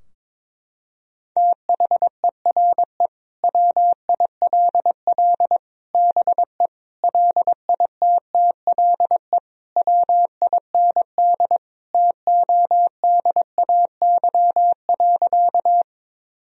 Morse code, heard as a keyed tone, about 22 wpm, 700 Hz. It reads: THERE WILL BE LITTLE WIND TODAY.
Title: Sentences from Top 1000 Words - Review Part 2 - 22wpm